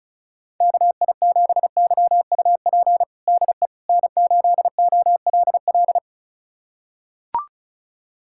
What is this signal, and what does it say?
Morse code, heard as a keyed tone, about 36 wpm, 700 Hz. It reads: KI7YUP DE N8OLL
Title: USA Callsign DE Callsign 35wpm